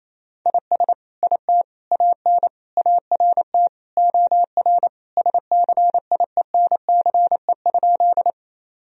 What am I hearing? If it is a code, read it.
Morse code, heard as a keyed tone, about 28 wpm, 700 Hz. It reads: IS IT AN ART OR SCIENCE?
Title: Sentences from Top 1000 Words - Review Part 3 - 28wpm